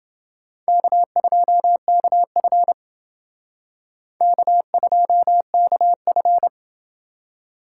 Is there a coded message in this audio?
Yes: Morse code from 0.7 to 6.5 s, heard as a keyed tone, about 28 wpm, 700 Hz. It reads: K2KF K2KF